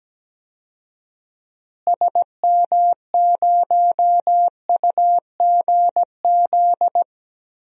Morse code, heard as a keyed tone, about 17 wpm, 700 Hz. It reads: SM0UGZ
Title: DXCC-100 Call Signs - 6 Characters - Repeated 3x - 17wpm